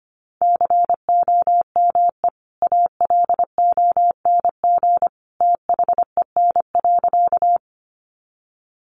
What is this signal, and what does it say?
Morse code, heard as a keyed tone, about 25 wpm, 700 Hz. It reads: COME ALONG THEN.
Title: Sentences from Top 700 Words 25wpm